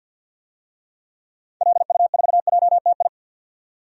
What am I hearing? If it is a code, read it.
Morse code, heard as a keyed tone, about 52 wpm, 700 Hz. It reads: LU4JTI
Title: DXCC-100 Call Signs - 6 Characters - 50wpm